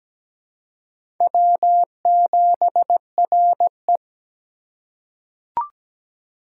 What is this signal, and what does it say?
Morse code, heard as a keyed tone, about 17 wpm, 700 Hz. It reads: W7RE
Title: USA Call Signs - 4 Characters - 3x Character Spacing - 17wpm